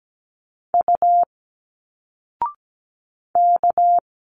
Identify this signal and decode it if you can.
Morse code, heard as a keyed tone, about 17 wpm, 700 Hz. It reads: U K
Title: Single Letters 17wpm